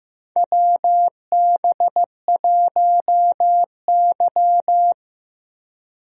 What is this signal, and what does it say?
Morse code, heard as a keyed tone, about 15 wpm, 700 Hz. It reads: WB1Y
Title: USA Call Signs - 4 Characters - Repeated 2x - 15wpm